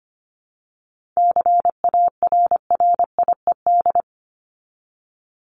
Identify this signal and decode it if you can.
Morse code, heard as a keyed tone, about 25 wpm, 700 Hz. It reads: CARRIED
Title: ED Suffix - Top 100 Words with Base Words 25wpm